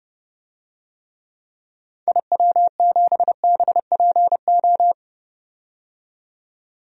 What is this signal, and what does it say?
Morse code, heard as a keyed tone, about 30 wpm, 700 Hz. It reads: IW7BPO
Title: DXCC-100 Call Signs - 6 Characters - 30wpm